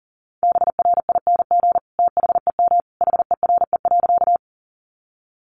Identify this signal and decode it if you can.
Morse code, heard as a keyed tone, about 40 wpm, 700 Hz. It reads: BRING THEM HERE.
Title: Sentences from Top 500 Words 40wpm